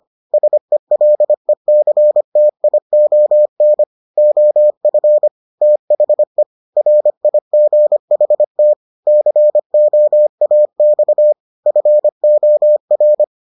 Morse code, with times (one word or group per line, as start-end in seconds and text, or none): 0.3-3.8 SELECTION
4.2-5.3 OF
5.6-6.4 THE
6.8-8.7 RIGHT
9.1-11.3 COAX
11.7-13.2 FOR